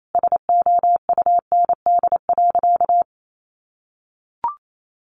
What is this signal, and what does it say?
Morse code, heard as a keyed tone, about 28 wpm, 690 Hz. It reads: SOUND.
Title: Sentences from Top 900 Words 28wpm